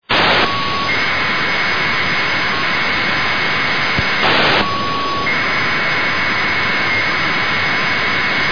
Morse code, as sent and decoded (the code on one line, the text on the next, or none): none